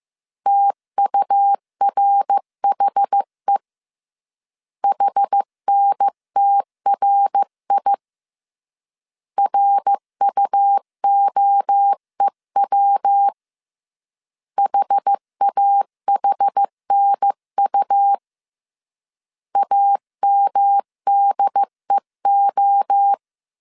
Morse code, heard as a keyed tone, about 15 wpm, 800 Hz. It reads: TURHE HNTRI RUOEW HAHNU AMDEO
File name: CW / DET / 1 3 1500C S A 14wpm 2447